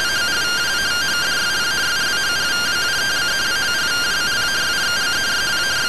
Two-tone teleprinter signal, about 170 Hz shift.